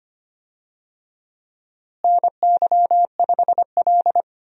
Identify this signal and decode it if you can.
Morse code, heard as a keyed tone, about 25 wpm, 700 Hz. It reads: NY5L
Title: USA Call Signs - 4 Characters - Repeated 2x - 25wpm